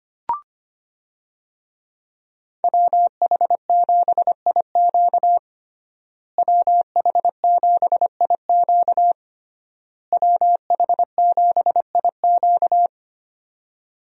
Morse code, heard as a keyed tone, about 25 wpm, 700 Hz. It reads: WH7IQ WH7IQ WH7IQ
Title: USA Call Signs - 5 Characters - Repeated 3x - 25wpm